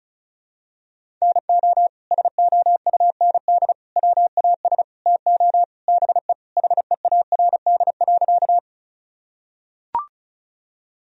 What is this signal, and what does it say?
Morse code, heard as a keyed tone, about 36 wpm, 690 Hz. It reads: NO SOUND WAS TO BE HEARD.
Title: Sentences from Top 1000 Words - Review Part 1 - 35wpm